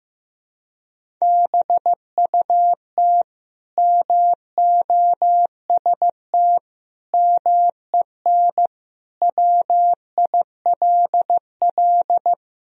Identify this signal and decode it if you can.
Morse code, heard as a keyed tone, about 15 wpm, 700 Hz. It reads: BUT MOST MEN WILL
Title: Sentences from Top 1000 Words - Review Part 1 - 15wpm